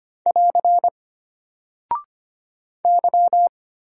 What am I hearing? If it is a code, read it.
Morse code, heard as a keyed tone, about 25 wpm, 700 Hz. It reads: <AR> Y
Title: Single Character (Letter:Number:Punctuation:Prosign) Alt 25wpm